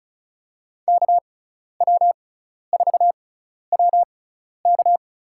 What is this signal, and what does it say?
Morse code, heard as a keyed tone, about 36 wpm, spaced as for 12 wpm, 700 Hz. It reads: KW4WK
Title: USA Call Signs - 5 Characters - 6x Character Spacing - 35wpm